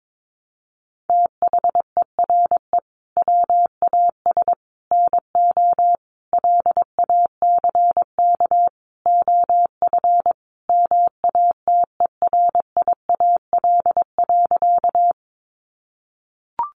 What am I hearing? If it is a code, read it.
Morse code, heard as a keyed tone, about 22 wpm, 700 Hz. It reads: THERE WAS NO LACK OF MATERIAL.
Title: Sentences from Top 1100 Words 22wpm